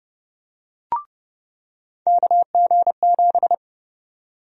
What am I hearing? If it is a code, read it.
Morse code, heard as a keyed tone, about 30 wpm, 700 Hz. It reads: KG7
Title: USA Call Sign Prefixes - 30wpm